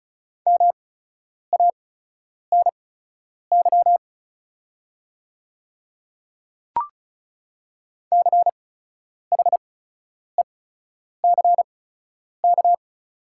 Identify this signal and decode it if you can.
Morse code, heard as a keyed tone, about 34 wpm, spaced as for 10 wpm, 690 Hz. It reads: MANY CHECK
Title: 8x Character Spacing - Top 500 Words - 35wpm